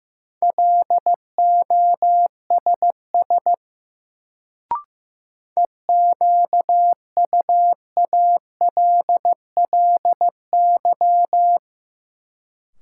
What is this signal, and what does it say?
Morse code, heard as a keyed tone, about 16 wpm, 700 Hz. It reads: LOSS EQUALLY
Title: Sentences from Top 1400 Words - New Word Drill Alt - 15wpm